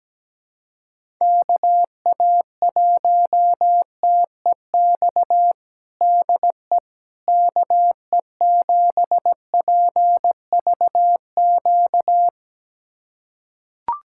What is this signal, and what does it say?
Morse code, heard as a keyed tone, about 17 wpm, 700 Hz. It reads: KA1TEX DE KE7PVQ
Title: USA Callsign DE Callsign 17wpm